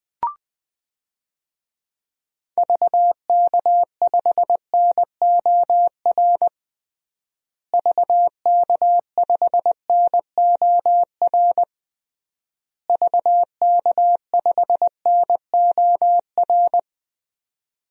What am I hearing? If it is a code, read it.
Morse code, heard as a keyed tone, about 20 wpm, 700 Hz. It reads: VK5NOR VK5NOR VK5NOR
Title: DXCC-100 Call Signs - 6 Characters - Repeated 3x - 20wpm